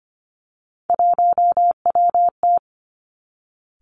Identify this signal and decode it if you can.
Morse code, heard as a keyed tone, about 25 wpm, 700 Hz. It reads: 1WT
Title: Three Challenging Letter-Number Combinations 25wpm